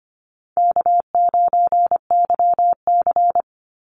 Morse code, heard as a keyed tone, about 25 wpm, 700 Hz. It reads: K9YC
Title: Easy Call Signs Repeated 3x - 25wpm